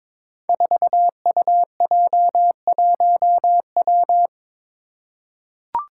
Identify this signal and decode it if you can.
Morse code, heard as a keyed tone, about 22 wpm, 700 Hz. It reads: 4UJ1W
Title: Five Challenging Letter-Number Combinations 22wpm